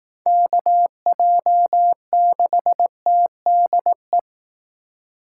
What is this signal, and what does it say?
Morse code, heard as a keyed tone, about 18 wpm, 700 Hz. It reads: KJ6TDE
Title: Speed Racing - USA 2x3 Call Signs 15wpm